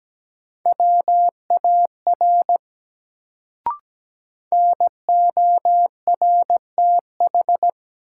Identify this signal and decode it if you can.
Morse code, heard as a keyed tone, about 17 wpm, 700 Hz. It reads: WAR NORTH
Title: Sentences from Top 400 Words - New Word Drill - 17wpm